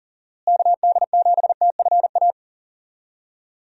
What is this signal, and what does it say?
Morse code, heard as a keyed tone, about 40 wpm, 700 Hz. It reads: KD7TFA
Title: USA Call Signs - 6 Characters - Repeated 3x - 40wpm